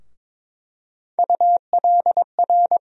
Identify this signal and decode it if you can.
Morse code, heard as a keyed tone, about 21 wpm, 700 Hz. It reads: ULR
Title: USA Call Sign Postfixes - 3 Letters 22wpm